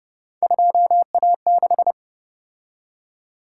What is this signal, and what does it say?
Morse code, heard as a keyed tone, about 29 wpm, 700 Hz. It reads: 2A6
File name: Speed-Racing - Three Letter-Number Combinations - 25wpm